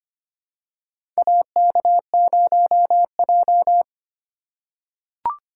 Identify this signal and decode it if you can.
Morse code, heard as a keyed tone, about 25 wpm, 700 Hz. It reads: AK0J
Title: USA Call Signs - 4 Characters - 4x Character Spacing - 25wpm